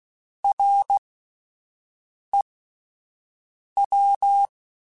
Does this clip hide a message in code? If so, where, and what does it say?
Yes: Morse code from 0.4 to 4.5 s, heard as a keyed tone, about 16 wpm, spaced as for 5 wpm, 800 Hz. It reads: REW